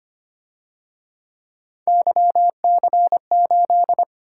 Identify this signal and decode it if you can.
Morse code, heard as a keyed tone, about 29 wpm, 700 Hz. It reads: YC8
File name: Speed-Racing - Three Letter-Number Combinations - 25wpm